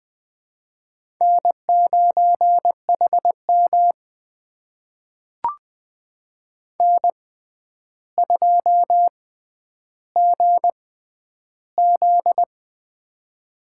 Morse code, heard as a keyed tone, about 20 wpm, spaced as for 10 wpm, 700 Hz. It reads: N9HM N2GZ
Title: USA Call Signs - 4 Characters - 6x Character Spacing - 20wpm